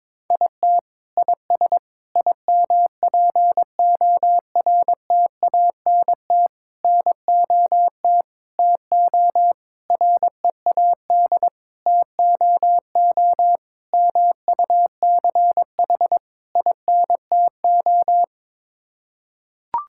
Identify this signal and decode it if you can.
Morse code, heard as a keyed tone, about 22 wpm, 700 Hz. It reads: IT IS IMPORTANT NOT TO READ TOO MUCH INTO
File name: Sets of 9 Words - 22wpm